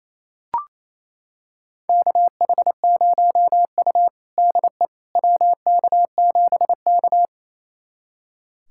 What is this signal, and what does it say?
Morse code, heard as a keyed tone, about 28 wpm, 700 Hz. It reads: KH0U DE WK7K
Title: USA Callsign DE Callsign - 4 Characters - 28wpm